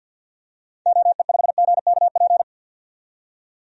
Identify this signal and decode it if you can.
Morse code, heard as a keyed tone, about 47 wpm, 700 Hz. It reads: OE5GKP
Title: DXCC-100 Call Signs - 6 Characters - 50wpm